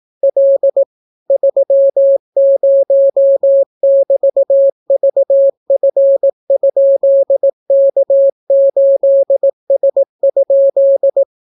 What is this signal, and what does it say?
Morse code, heard as a keyed tone, about 18 wpm, 550 Hz. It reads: L 30<BT>VF?K8S?